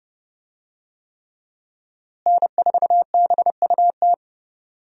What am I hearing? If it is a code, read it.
Morse code, heard as a keyed tone, about 31 wpm, 700 Hz. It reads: N4BUT
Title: USA Call Signs - 5 Characters - Repeated 3x - 30wpm